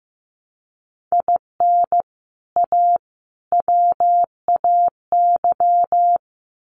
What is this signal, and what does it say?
Morse code, heard as a keyed tone, about 15 wpm, 700 Hz. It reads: IN A WAY
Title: Sets of 3 Words 15wpm